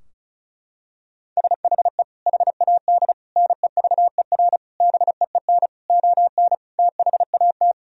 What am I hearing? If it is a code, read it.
Morse code, heard as a keyed tone, about 36 wpm, 700 Hz. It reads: SHE HAD NEVER BEEN ON THAT
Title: Sentences from Top 1000 Words - Review Part 2 - 35wpm